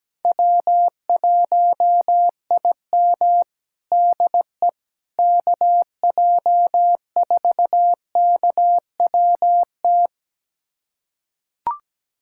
Morse code, heard as a keyed tone, about 17 wpm, 700 Hz. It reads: W1IM DE KJ4KWT
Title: USA Callsign DE Callsign 17wpm